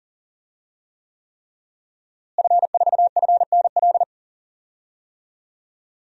Morse code, heard as a keyed tone, about 40 wpm, 700 Hz. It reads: F4FNL
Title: DXCC-100 Call Signs - 5 Characters - Repeated 3x - 40wpm